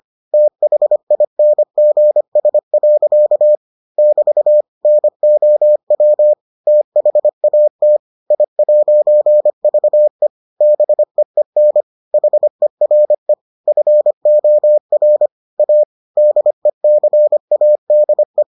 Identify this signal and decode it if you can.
Morse code, heard as a keyed tone, about 25 wpm, 600 Hz. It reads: THINGS. <BT> NOW THAT I'VE BEEN HERE FOR A DECADE